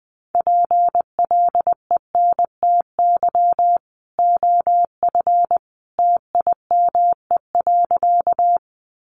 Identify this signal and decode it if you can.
Morse code, heard as a keyed tone, about 19 wpm, 700 Hz. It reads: PLENTY OF TIME.
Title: Sentences from Top 1300 Words 20wpm